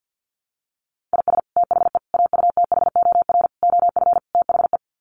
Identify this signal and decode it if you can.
Morse code, heard as a keyed tone, about 50 wpm, 700 Hz. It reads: IS THE AUTHOR OF THE
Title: Sets of 5 Words Encore 50wpm